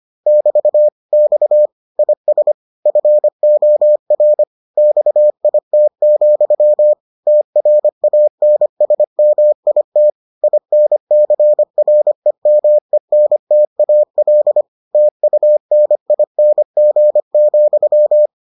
Morse code, heard as a keyed tone, about 25 wpm, 600 Hz. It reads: <BT> X IS FOR XIT, TRANSMIT INCREMENTAL TUNING,